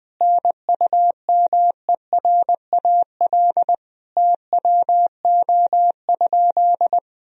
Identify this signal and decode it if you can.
Morse code, heard as a keyed tone, about 20 wpm, 700 Hz. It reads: NUMERAL TWO?
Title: Sentences from Top 1000 Words - Review Part 2 - 20wpm